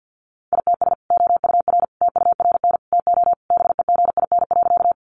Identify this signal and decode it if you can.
Morse code, heard as a keyed tone, about 50 wpm, 700 Hz. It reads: ITS OUR TURN TO BEGIN.
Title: Sentences from Top 600 Words 50wpm